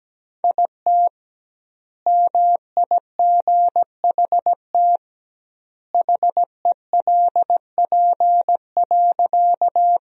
Morse code, heard as a keyed tone, about 17 wpm, 700 Hz. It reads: IT MIGHT HELP.
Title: 2x Word-Spacing - Sentences from Top 300 Words - 17wpm